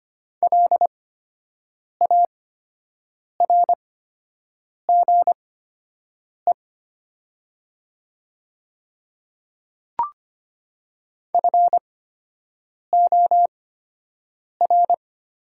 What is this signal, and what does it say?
Morse code, heard as a keyed tone, about 25 wpm, spaced as for 7 wpm, 700 Hz. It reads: LARGE FOR
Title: 8x Character Spacing - Top 300 Words - 25wpm